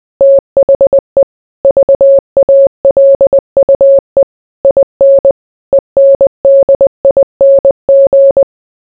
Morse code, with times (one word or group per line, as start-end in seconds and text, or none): 0.2-1.2 THE
1.6-4.2 VALUE
4.6-5.3 IN
5.7-8.4 ENDING